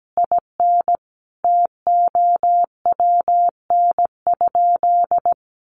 Morse code, heard as a keyed tone, about 17 wpm, 700 Hz. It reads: IN TOWN?
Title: Sentences from Top 1200 Words 17wpm